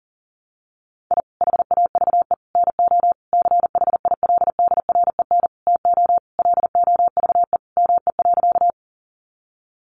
Morse code, heard as a keyed tone, about 37 wpm, 700 Hz. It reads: I HAVE NO CHILDREN TO LOVE ME.
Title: Sentences from Top 400 Words 40wpm